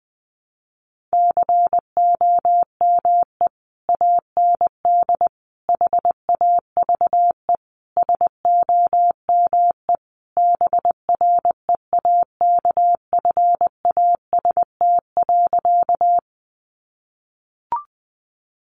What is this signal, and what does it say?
Morse code, heard as a keyed tone, about 20 wpm, 700 Hz. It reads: COME AND HAVE SOME BREAKFAST.
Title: Sentences from Top 1500 Words 20wpm